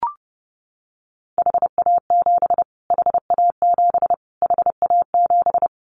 Morse code, heard as a keyed tone, about 30 wpm, 700 Hz. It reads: HA7 HA7 HA7
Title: Callsign Prefixes for Top 25 DXCC - Repeated 3x - 30wpm